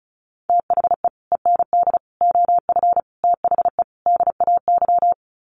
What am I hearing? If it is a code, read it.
Morse code, heard as a keyed tone, about 34 wpm, 700 Hz. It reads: THE END OF THE DAY
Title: Sets of 5 Words 35wpm